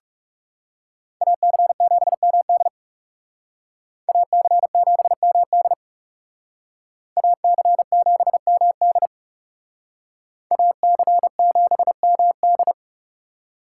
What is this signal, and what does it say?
Morse code, heard as a keyed tone, about 38 wpm, 700 Hz. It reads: AC7MD AC7MD AC7MD AC7MD